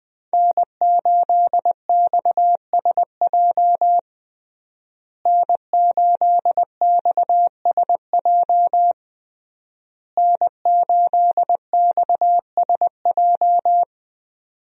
Morse code, heard as a keyed tone, about 20 wpm, 700 Hz. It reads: N8XSJ N8XSJ N8XSJ
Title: USA Call Signs - 5 Characters - Repeated 3x - 20wpm